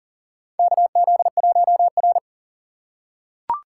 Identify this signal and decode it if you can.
Morse code, heard as a keyed tone, about 40 wpm, 690 Hz. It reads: KZ1R
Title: USA Call Signs - 4 Characters - 2x Character Spacing - 40wpm